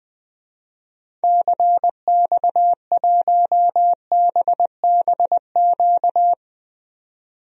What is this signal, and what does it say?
Morse code, heard as a keyed tone, about 20 wpm, 700 Hz. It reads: CX1BBQ
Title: DXCC-100 Call Signs - 6 Characters - 20wpm